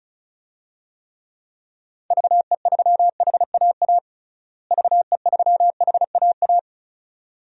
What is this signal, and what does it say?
Morse code, heard as a keyed tone, about 34 wpm, 700 Hz. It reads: VE3HAA VE3HAA
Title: DXCC-100 Call Signs - 6 Characters - Repeated 2x - 35wpm